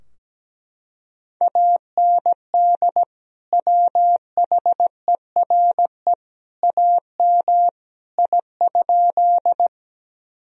Morse code, heard as a keyed tone, about 17 wpm, 700 Hz. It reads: AND WHERE AM I?
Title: Sentences from Top 1000 Words - Review Part 2 - 17wpm